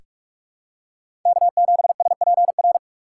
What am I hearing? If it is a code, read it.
Morse code, heard as a keyed tone, about 46 wpm, 700 Hz. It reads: K7SPR
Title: USA Call Signs - 5 Characters - 4x Character Spacing - 45wpm